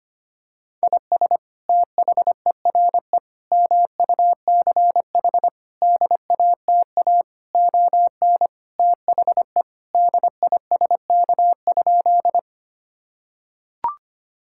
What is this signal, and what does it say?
Morse code, heard as a keyed tone, about 25 wpm, 700 Hz. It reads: IS THERE MUCH DATA ON THE DISK?
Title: Sentences from Top 1000 Words - Review Part 6 - 25wpm